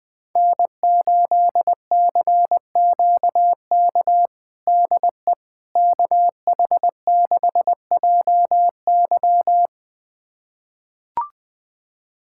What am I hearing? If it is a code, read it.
Morse code, heard as a keyed tone, about 20 wpm, 700 Hz. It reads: N8CQK DE KH6JY
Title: USA Callsign DE Callsign - 5 Characters - 20wpm